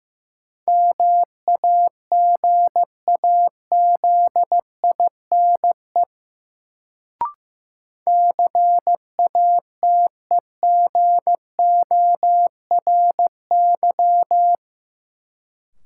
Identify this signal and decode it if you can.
Morse code, heard as a keyed tone, about 16 wpm, 700 Hz. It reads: MAGAZINE CATEGORY
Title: Sentences from Top 1000 Words - Review Part 7 - New Word Drill - 15wpm